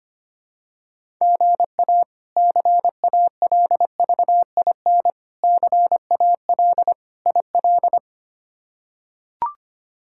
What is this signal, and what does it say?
Morse code, heard as a keyed tone, about 25 wpm, 700 Hz. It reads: GA CALVIN CAL IL
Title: K1USN SST QSO Practice - 25wpm